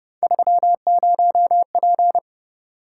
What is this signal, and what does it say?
Morse code, heard as a keyed tone, about 29 wpm, 700 Hz. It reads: 30P